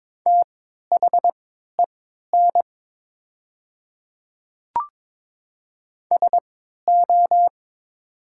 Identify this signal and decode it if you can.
Morse code, heard as a keyed tone, about 22 wpm, spaced as for 12 wpm, 700 Hz. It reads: THEN SO
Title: 3x Character Spacing - Top 100 Words - 22wpm